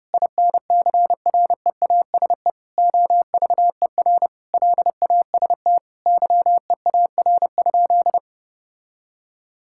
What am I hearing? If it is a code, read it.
Morse code, heard as a keyed tone, about 30 wpm, 700 Hz. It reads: INCREASE OVER LAST YEAR?
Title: Sentences from Top 1000 Words - Review Part 5 - 30wpm